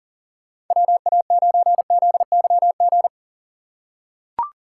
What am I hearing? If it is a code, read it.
Morse code, heard as a keyed tone, about 38 wpm, 700 Hz. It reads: WA9ZYG
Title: USA Call Signs - 6 Characters - 6x Character Spacing - 40wpm